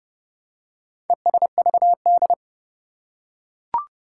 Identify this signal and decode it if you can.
Morse code, heard as a keyed tone, about 29 wpm, 700 Hz. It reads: ESVD
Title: Four Challenging Letter-Number Combinations 30wpm